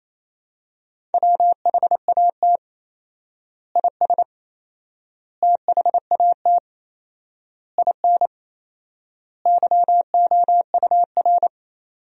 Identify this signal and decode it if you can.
Morse code, heard as a keyed tone, about 27 wpm, 700 Hz. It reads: WHAT IS THAT IN YOUR
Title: 4x Word-Spacing - Sentences from Top 200 Words - 28wpm